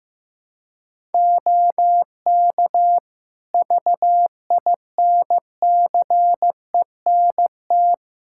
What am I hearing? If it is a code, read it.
Morse code, heard as a keyed tone, about 15 wpm, 700 Hz. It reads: OK VINCENT
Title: Practice QSO - Exchange 3 - More Info - 15wpm